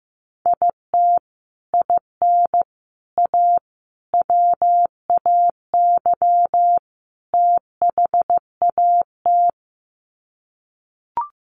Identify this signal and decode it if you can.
Morse code, heard as a keyed tone, about 15 wpm, 700 Hz. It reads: IT IN A WAY THAT
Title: Sets of 5 Words Encore2 15wpm